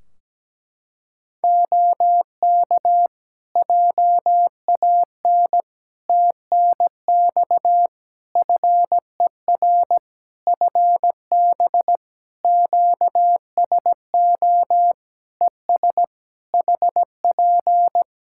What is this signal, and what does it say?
Morse code, heard as a keyed tone, about 17 wpm, 700 Hz. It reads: OK JAN TNX FER FB QSO ES HP
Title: K1USN SST QSO Practice 17wpm